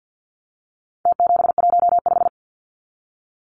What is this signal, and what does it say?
Morse code, heard as a keyed tone, about 49 wpm, 700 Hz. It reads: T715
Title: Numbers - Contest Style - 4-Digit 50wpm